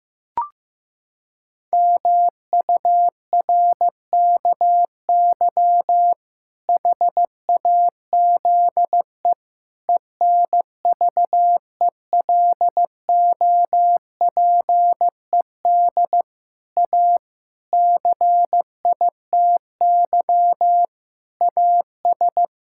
Morse code, heard as a keyed tone, about 15 wpm, 700 Hz. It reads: MURKY HAZE ENVELOPED A CITY AS
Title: Pangram Challenge v2 - 15wpm